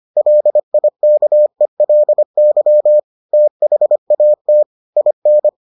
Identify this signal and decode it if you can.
Morse code, heard as a keyed tone, about 25 wpm, 600 Hz. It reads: LIKELY THAT IN